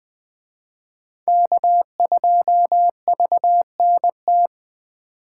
Morse code, heard as a keyed tone, about 20 wpm, 700 Hz. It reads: K2VNT
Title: Speed Racing - USA 1x3 Call Signs 17wpm